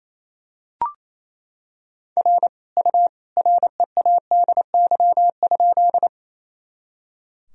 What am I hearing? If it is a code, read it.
Morse code, heard as a keyed tone, about 28 wpm, 700 Hz. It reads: R U READY?
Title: Sentences from Top 400 Words and Abbreviations 28wpm